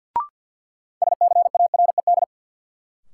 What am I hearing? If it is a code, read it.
Morse code, heard as a keyed tone, about 47 wpm, 700 Hz. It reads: SCARED
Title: Sentences from Top 700 Words - New Word Drill - 50wpm